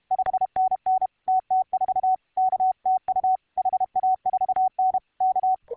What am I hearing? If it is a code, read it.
Morse code, heard as a keyed tone, about 31 wpm, 750 Hz. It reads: 5NN TT4 K TU HA4N K